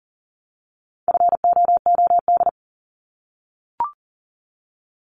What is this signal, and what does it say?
Morse code, heard as a keyed tone, about 40 wpm, 700 Hz. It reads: FOOD
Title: Multi-letter Phonemes Repeated 3x 40wpm